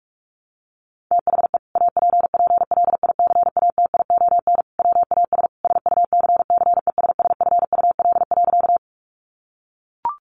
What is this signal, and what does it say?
Morse code, heard as a keyed tone, about 44 wpm, 700 Hz. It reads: THE APPLICATION WAS SUCCESSFUL.
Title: Sentences from Top 1200 Words 45wpm